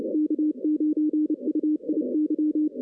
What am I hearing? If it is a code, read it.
Morse code, heard as a keyed tone, about 31 wpm, 300 Hz, with radio noise. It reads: K9UIY
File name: 2016 arrl dx cw / 2016-ARRL-DX-CW-K5TR- - 0-34